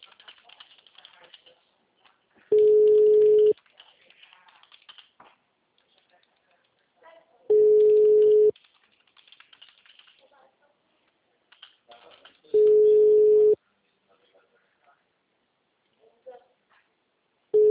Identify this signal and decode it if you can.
Ringback tone (Europe, 425 Hz), with speech.